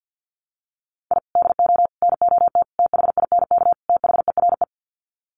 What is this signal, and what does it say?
Morse code, heard as a keyed tone, about 50 wpm, 700 Hz. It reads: I DO NOT THINK THERE
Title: Sets of 5 Words 50wpm